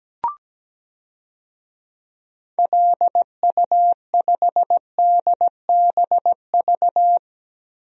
Morse code, heard as a keyed tone, about 17 wpm, 700 Hz. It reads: LU5DBV
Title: DXCC-100 Call Signs - 6 Characters - Repeated 3x - 17wpm